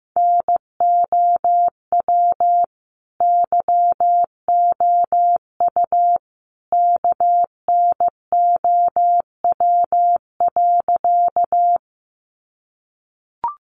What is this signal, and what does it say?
Morse code, heard as a keyed tone, about 15 wpm, 700 Hz. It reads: NOW YOU KNOW.
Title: Sentences from Top 100 Words 15wpm